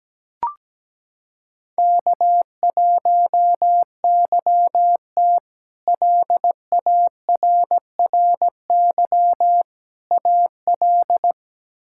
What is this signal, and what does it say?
Morse code, heard as a keyed tone, about 17 wpm, 700 Hz. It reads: K1YT LARRY AL
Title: K1USN SST QSO Practice - 17wpm